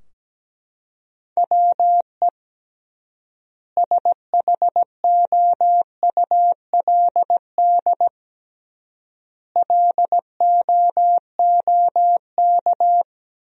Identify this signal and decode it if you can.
Morse code, heard as a keyed tone, about 17 wpm, 700 Hz. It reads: WE SHOULD LOOK
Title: 3x Word-Spacing - Sentences from Top 300 Words - 17wpm